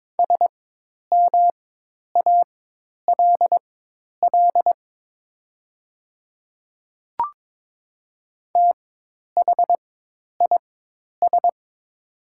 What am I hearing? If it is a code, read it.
Morse code, heard as a keyed tone, about 22 wpm, spaced as for 10 wpm, 700 Hz. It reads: SMALL THIS
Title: 4x Character Spacing - Top 200 Words - 22wpm